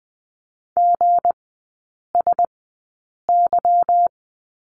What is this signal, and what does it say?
Morse code, heard as a keyed tone, about 20 wpm, spaced as for 8 wpm, 690 Hz. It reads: GSY
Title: 159 Lesson 26e - Letter g - Three Characters 20wpm